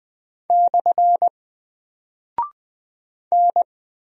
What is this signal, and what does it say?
Morse code, heard as a keyed tone, about 20 wpm, 700 Hz. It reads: / N
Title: Single Character (Letter:Number:Punctuation:Prosign) 20wpm